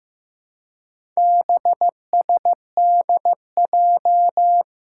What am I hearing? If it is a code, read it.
Morse code, heard as a keyed tone, about 15 wpm, 700 Hz. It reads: BSDJ
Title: Four Challenging Letter-Number Combinations 15wpm